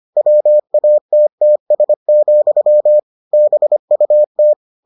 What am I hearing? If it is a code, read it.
Morse code, heard as a keyed tone, about 25 wpm, 600 Hz. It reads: WATTS, BUT